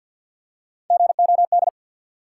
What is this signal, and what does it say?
Morse code, heard as a keyed tone, about 53 wpm, 700 Hz. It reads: GOD
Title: Three Letter Words - 50wpm